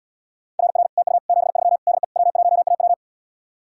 Morse code, heard as a keyed tone, about 80 wpm, 700 Hz. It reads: RR TU 73 DE W1AW